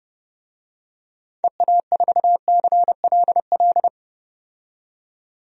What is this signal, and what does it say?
Morse code, heard as a keyed tone, about 30 wpm, 700 Hz. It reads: EA4CLL
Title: DXCC-100 Call Signs - 6 Characters - 30wpm